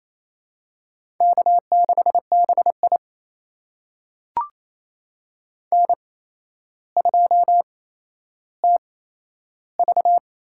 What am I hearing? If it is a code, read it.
Morse code, heard as a keyed tone, about 28 wpm, spaced as for 12 wpm, 700 Hz. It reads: K6BI N2TV
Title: USA Call Signs - 4 Characters - 8x Character Spacing - 28wpm